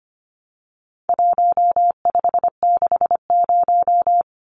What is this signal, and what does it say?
Morse code, heard as a keyed tone, about 25 wpm, 700 Hz. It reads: 1560
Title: Numbers - Contest Style - 4-Digit 25wpm